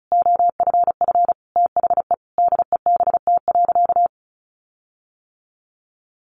Morse code, heard as a keyed tone, about 34 wpm, 700 Hz. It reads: OFF THE DEBT.
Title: Sentences from Top 1300 Words 35wpm